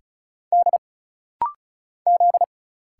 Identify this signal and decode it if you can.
Morse code, heard as a keyed tone, about 36 wpm, 700 Hz. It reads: D Z
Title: Single Letters - Rapid-Fire 35wpm